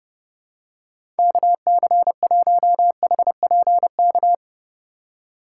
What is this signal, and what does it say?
Morse code, heard as a keyed tone, about 32 wpm, 700 Hz. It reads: KC1HPK
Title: USA Call Signs - 6 Characters - 2x Character Spacing - 30wpm